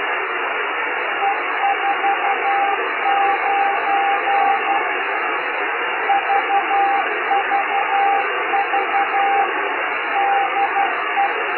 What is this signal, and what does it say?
Morse code, heard as a keyed tone, about 11 wpm, 800 Hz, with radio noise. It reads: E49 VVV DE
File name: Beacon / FujisanPJ / 8J1FUJ 2022